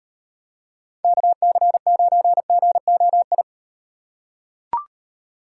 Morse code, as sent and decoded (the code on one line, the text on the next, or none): -.- -.-. ----. --. --- ..
KC9GOI